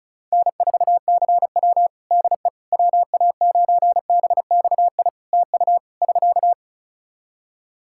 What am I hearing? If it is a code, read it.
Morse code, heard as a keyed tone, about 34 wpm, 700 Hz. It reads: N4CW DE WA9BXI TU <SK>
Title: Practice QSO - Exchange 4 - Ending - 35wpm